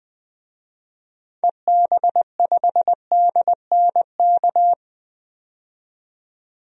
Morse code, heard as a keyed tone, about 20 wpm, 700 Hz. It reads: EB5DNK
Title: DXCC-100 Call Signs - 6 Characters - 20wpm